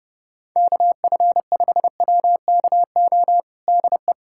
Morse code, heard as a keyed tone, about 30 wpm, 700 Hz. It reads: KF5WKO DE
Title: USA Callsign DE Callsign 30wpm